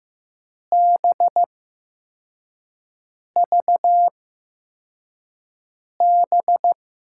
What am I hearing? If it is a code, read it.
Morse code, heard as a keyed tone, about 15 wpm, spaced as for 4 wpm, 700 Hz. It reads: BVB
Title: USA Call Signs - 6 Characters - 8x Character Spacing - 15wpm